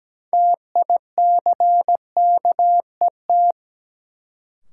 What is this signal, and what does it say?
Morse code, heard as a keyed tone, about 17 wpm, 700 Hz. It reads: TICKET
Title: Sentences from Top 1400 Words - New Word Drill Alt - 17wpm